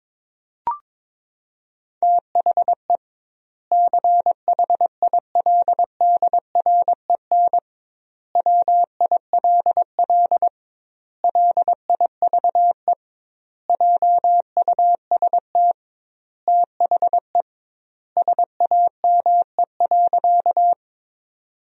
Morse code, heard as a keyed tone, about 21 wpm, 700 Hz. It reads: THE CHILDREN WILL LIVE JUST THE SAME.
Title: 2x Word-Spacing - Sentences from Top 300 Words - 22wpm